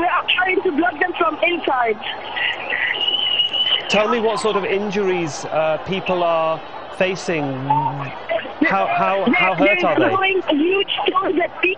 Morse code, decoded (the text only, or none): none